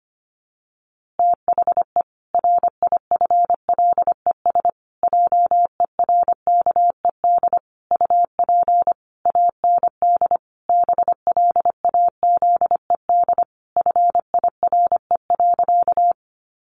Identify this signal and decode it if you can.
Morse code, heard as a keyed tone, about 25 wpm, 700 Hz. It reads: THE RIFLES JERKED UP AND BLAZED FIRE.
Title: A Sound Of Thunder 25wpm